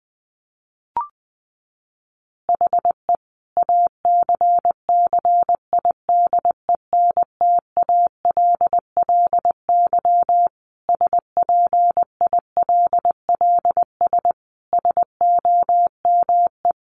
Morse code, heard as a keyed tone, about 19 wpm, 700 Hz. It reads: HE ACCIDENTALLY SPILLS SOME
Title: Additional Easy Phrases 20wpm